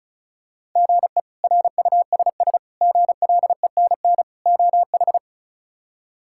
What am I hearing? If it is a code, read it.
Morse code, heard as a keyed tone, about 34 wpm, 700 Hz. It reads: GE RUSS GLENN OH
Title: K1USN SST QSO Practice - 35wpm